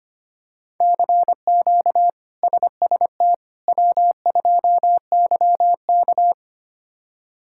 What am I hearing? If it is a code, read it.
Morse code, heard as a keyed tone, about 25 wpm, 700 Hz. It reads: CQ SST W2YK
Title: K1USN SST QSO Practice - 25wpm